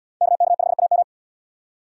Morse code, heard as a keyed tone, about 76 wpm, 700 Hz. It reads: KK5AM